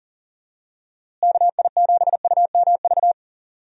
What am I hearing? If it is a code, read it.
Morse code, heard as a keyed tone, about 39 wpm, 700 Hz. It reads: KI7UMV
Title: USA Call Signs - 6 Characters - Repeated 3x - 40wpm